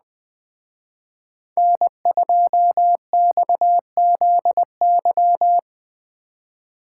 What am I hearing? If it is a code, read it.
Morse code, heard as a keyed tone, about 21 wpm, 700 Hz. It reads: N2XZY
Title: Speed Racing - USA 1x3 Call Signs 17wpm